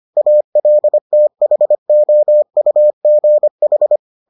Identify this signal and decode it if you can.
Morse code, heard as a keyed tone, about 25 wpm, 600 Hz. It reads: ALTHOUGH